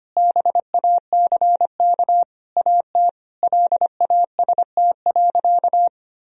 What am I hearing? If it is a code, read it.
Morse code, heard as a keyed tone, about 25 wpm, 700 Hz. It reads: BACK AT LAST.
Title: Sentences from Top 1000 Words - Review Part 2 - 25wpm